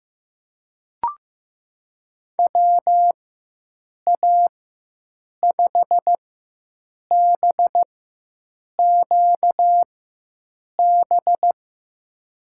Morse code, heard as a keyed tone, about 15 wpm, spaced as for 7 wpm, 700 Hz. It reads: WA5BQB
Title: USA Call Signs - 4x Character Spacing - 15wpm